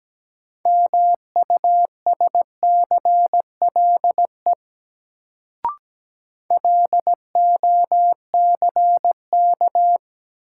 Morse code, heard as a keyed tone, about 17 wpm, 700 Hz. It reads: MUSCLE LOCK
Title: Sentences from Top 1300 Words - New Word Drill Alt - 17wpm